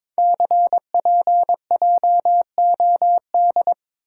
Morse code, heard as a keyed tone, about 22 wpm, 700 Hz. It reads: CPJOD
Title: Code Groups - 5 Letters - 22wpm